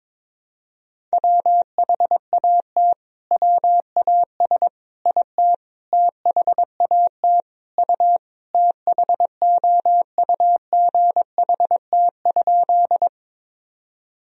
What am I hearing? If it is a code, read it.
Morse code, heard as a keyed tone, about 22 wpm, 700 Hz. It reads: WHAT WAS IT THAT U THOUGHT?
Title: Sentences from Top 300 Words and Abbreviations 22wpm